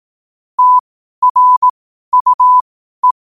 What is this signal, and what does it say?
Morse code, heard as a keyed tone, about 18 wpm, spaced as for 12 wpm, 1000 Hz. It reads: TRUE
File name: qotdfiles / 12wpm